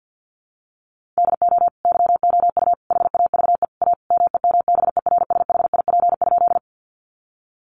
Morse code, heard as a keyed tone, about 50 wpm, 700 Hz. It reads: DO YOU HAVE A MEMBERSHIP?
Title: Sentences from Top 1400 Words 50wpm